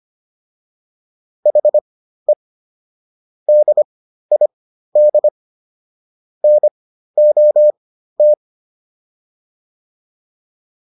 Morse code, heard as a keyed tone, about 25 wpm, spaced as for 13 wpm, 600 Hz. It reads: HE DID NOT